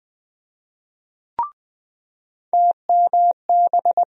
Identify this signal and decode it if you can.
Morse code, heard as a keyed tone, about 20 wpm, 700 Hz. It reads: TMB
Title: USA Call Sign Postfixes - 3 Letters 20wpm